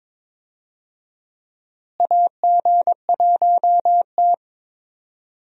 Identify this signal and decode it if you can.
Morse code, heard as a keyed tone, about 22 wpm, 700 Hz. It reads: AG1T
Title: USA Call Signs - 4 Characters - Repeated 3x - 22wpm